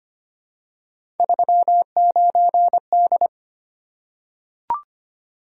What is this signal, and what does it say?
Morse code, heard as a keyed tone, about 25 wpm, 700 Hz. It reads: 39D
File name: Three Letter-Number Combinations - 25wpm